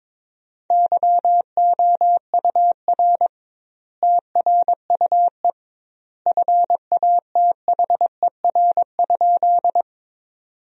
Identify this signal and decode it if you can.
Morse code, heard as a keyed tone, about 22 wpm, 700 Hz. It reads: YOUR TRUE FATHER?
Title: 2x Word-Spacing - Sentences from Top 400 Words - 22wpm